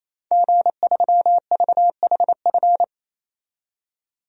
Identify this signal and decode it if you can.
Morse code, heard as a keyed tone, about 28 wpm, 700 Hz. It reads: G3VHF
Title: DXCC-100 Call Signs - 5 Characters - 28wpm